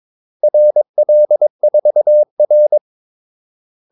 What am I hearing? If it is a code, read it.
Morse code, heard as a keyed tone, about 22 wpm, 600 Hz. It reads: RL4R
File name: Callsigns-22wpm